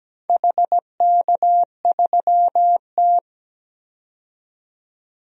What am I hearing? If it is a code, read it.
Morse code, heard as a keyed tone, about 17 wpm, 700 Hz. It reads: HK3T
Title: DXCC-100 Call Signs - 4 Characters - 17wpm